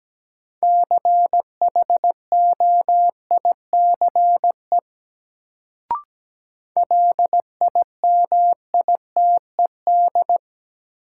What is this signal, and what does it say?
Morse code, heard as a keyed tone, about 17 wpm, 700 Hz. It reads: CHOICE LIMITED
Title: Sentences from Top 900 Words - New Word Drill - 17wpm